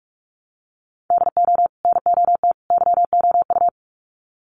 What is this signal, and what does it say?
Morse code, heard as a keyed tone, about 44 wpm, 700 Hz. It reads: DO NOT YOU
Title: Sets of 3 Words 45wpm